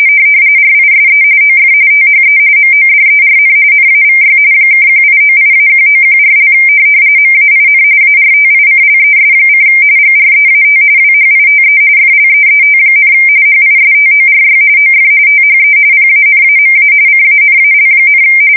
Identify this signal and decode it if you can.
RTTY, about 45.45 baud. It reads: RISE OF THE MODERN NAVY (1898-1914) MAIN ARTICLES: GREAT WHITE FLEET, HISTORY OF THE PANAMA CANAL,
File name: fsk170 / usnavy3